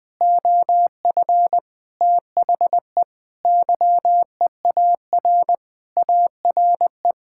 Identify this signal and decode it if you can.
Morse code, heard as a keyed tone, about 20 wpm, 700 Hz. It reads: OF THE YEAR ARE
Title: Sentences with 7-Word Phrases - 20wpm